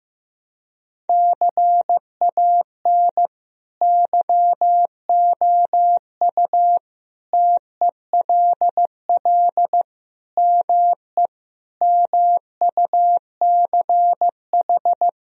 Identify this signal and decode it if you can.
Morse code, heard as a keyed tone, about 16 wpm, 700 Hz. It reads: CAN YOU TELL ME MUCH
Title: Sentences from Top 1000 Words - Review Part 4 - 15wpm